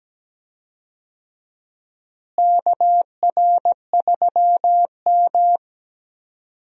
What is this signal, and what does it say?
Morse code, heard as a keyed tone, about 17 wpm, 700 Hz. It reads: KR3M
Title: USA Call Signs - 4 Characters - Repeated 3x - 17wpm